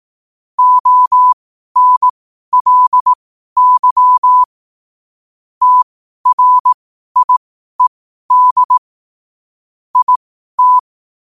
Morse code, heard as a keyed tone, about 18 wpm, spaced as for 12 wpm, 1000 Hz. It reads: ONLY TRIED IT
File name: qotdfiles / 12wpm